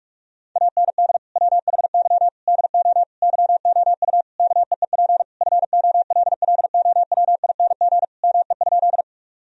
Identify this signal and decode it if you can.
Morse code, heard as a keyed tone, about 43 wpm, 700 Hz. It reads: AND WHY DO YOU KEEP FOLLOWING ME?
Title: Sentences from Top 1000 Words - Review Part 4 - 45wpm